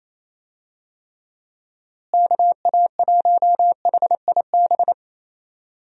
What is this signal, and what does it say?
Morse code, heard as a keyed tone, about 28 wpm, 700 Hz. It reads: KA1HIB
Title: USA Call Signs - 6 Characters - Repeated 2x - 28wpm